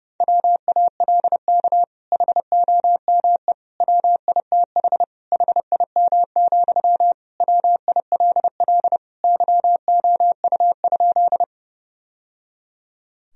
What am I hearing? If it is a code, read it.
Morse code, heard as a keyed tone, about 30 wpm, 700 Hz. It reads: WALK HOME WITH HIM, WILL YOU?
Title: Sentences from Top 1000 Words - Review Part 1 - 30wpm